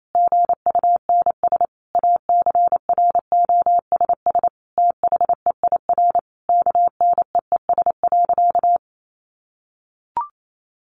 Morse code, heard as a keyed tone, about 28 wpm, 700 Hz. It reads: GUNS ACROSS THEIR KNEES.
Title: A Sound Of Thunder 28wpm